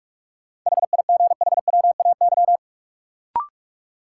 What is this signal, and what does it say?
Morse code, heard as a keyed tone, about 45 wpm, 700 Hz. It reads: HIGHWAY
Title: Sentences from Top 1400 Words - New Word Drill - 45wpm